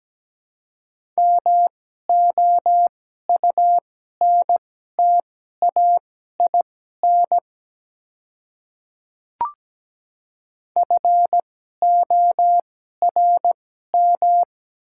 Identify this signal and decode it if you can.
Morse code, heard as a keyed tone, about 17 wpm, spaced as for 12 wpm, 700 Hz. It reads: MOUNTAIN FORM
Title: 2x Character Spacing - Top 300 Words - 17wpm